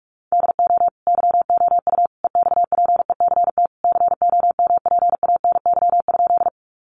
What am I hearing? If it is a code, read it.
Morse code, heard as a keyed tone, about 44 wpm, 700 Hz. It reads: DO YOU EXPECT COMPANY?
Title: Sentences from Top 1100 Words 45wpm